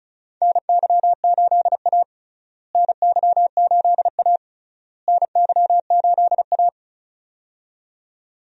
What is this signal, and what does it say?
Morse code, heard as a keyed tone, about 36 wpm, 700 Hz. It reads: NY8A NY8A NY8A
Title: USA Call Signs - 4 Characters - Repeated 3x - 35wpm